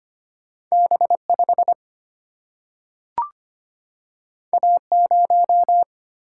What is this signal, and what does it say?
Morse code, heard as a keyed tone, about 25 wpm, 700 Hz. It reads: B 5 A 0
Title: Single-Letter Single-Number 25wpm